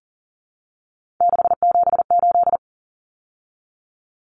Morse code, heard as a keyed tone, about 40 wpm, 700 Hz. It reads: 678
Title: Common Cut Numbers - 3-Digit 40wpm